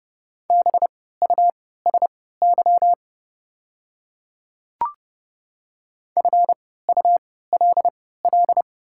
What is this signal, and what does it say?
Morse code, heard as a keyed tone, about 30 wpm, spaced as for 17 wpm, 690 Hz. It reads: BUSY FULL
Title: 3x Character Spacing - Top 500 Words - 30wpm